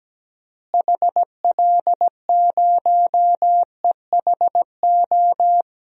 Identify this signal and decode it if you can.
Morse code, heard as a keyed tone, about 17 wpm, 700 Hz. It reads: HL0EHO
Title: DXCC-100 Call Signs - 6 Characters - Repeated 3x - 17wpm